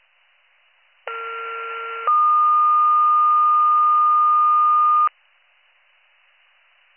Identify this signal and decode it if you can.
Two-tone page, heard as a one-second tone, then a three-second tone, about 1370 then 1220 Hz. It is heard from 1.1 to 5.1 s.